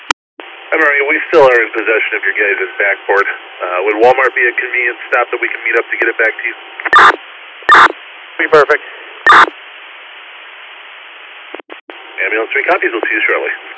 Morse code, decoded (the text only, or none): none